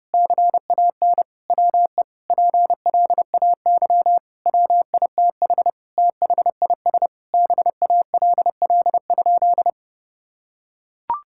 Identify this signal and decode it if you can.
Morse code, heard as a keyed tone, about 30 wpm, 700 Hz. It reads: CAN WE PLAY WITH THIS BALL?
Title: Sentences from Top 1000 Words - Review Part 6 - 30wpm